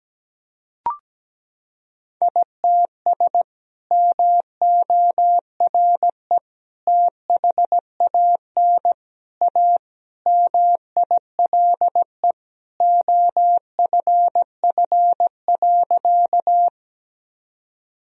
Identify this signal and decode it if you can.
Morse code, heard as a keyed tone, about 17 wpm, 700 Hz. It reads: ITS MORE THAN A MILE OFF.
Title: Sentences from Top 1000 Words - Review Part 1 - 17wpm